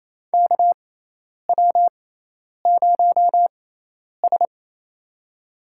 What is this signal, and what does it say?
Morse code, heard as a keyed tone, about 28 wpm, spaced as for 10 wpm, 700 Hz. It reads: KW0S
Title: USA Call Signs - 6x Character Spacing - 28wpm